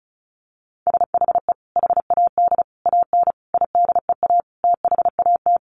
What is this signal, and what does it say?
Morse code, heard as a keyed tone, about 34 wpm, 700 Hz. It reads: SHE HAD AN IDEA THAT
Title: Sentences from Top 1400 Words 35wpm